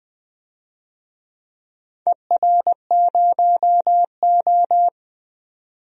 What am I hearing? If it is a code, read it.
Morse code, heard as a keyed tone, about 20 wpm, 700 Hz. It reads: ER0O
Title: DXCC-100 Call Signs - 4 Characters - 20wpm